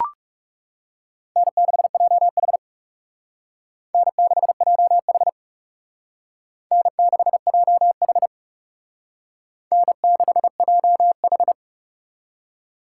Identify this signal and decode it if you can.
Morse code, heard as a keyed tone, about 38 wpm, 700 Hz. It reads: N6JH N6JH N6JH N6JH
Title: Speed Racing - USA 1x2 Call Signs 30wpm